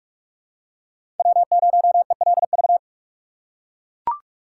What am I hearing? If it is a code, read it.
Morse code, heard as a keyed tone, about 44 wpm, 700 Hz. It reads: W0ELV
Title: Code Groups - 5 Letter:Numbers - 45wpm